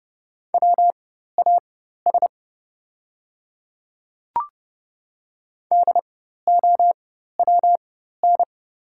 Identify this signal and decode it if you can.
Morse code, heard as a keyed tone, about 31 wpm, spaced as for 14 wpm, 700 Hz. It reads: WAS DOWN
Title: 4x Character Spacing - Top 100 Words - 30wpm